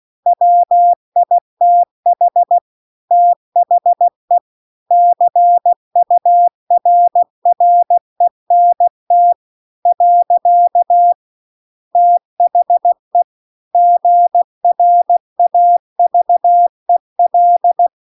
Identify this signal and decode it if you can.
Morse code, heard as a keyed tone, about 16 wpm, 700 Hz. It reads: WITH THE CURRENT . THE GRAVEL